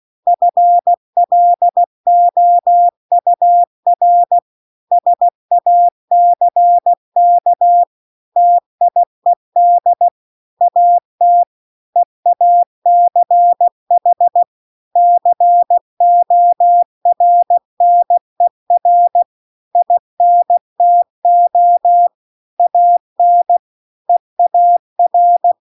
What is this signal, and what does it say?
Morse code, heard as a keyed tone, about 16 wpm, 700 Hz. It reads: FLOUR SACK TIED AT EACH CORNER INTO AN EAR
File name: CW-The-Big-Twoheaded-River-2-16WPM